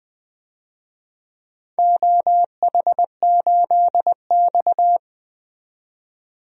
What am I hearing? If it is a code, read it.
Morse code, heard as a keyed tone, about 20 wpm, 700 Hz. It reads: OH8X
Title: DXCC-100 Call Signs - 4 Characters - 20wpm